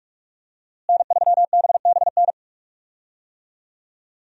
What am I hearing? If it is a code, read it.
Morse code, heard as a keyed tone, about 46 wpm, 700 Hz. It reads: N3BBN
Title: USA Call Signs - 5 Characters - Repeated 2x - 45wpm